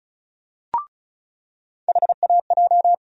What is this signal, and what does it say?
Morse code, heard as a keyed tone, about 36 wpm, 700 Hz. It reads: HAJ
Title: USA Call Sign Postfixes - 3 Letters 35wpm